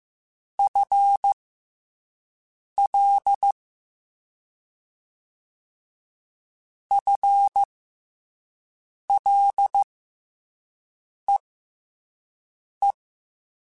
Morse code, heard as a keyed tone, about 15 wpm, spaced as for 5 wpm, 800 Hz. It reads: FL FLEE
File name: FL / FL mp3s / FL Prefixes 3x2 @5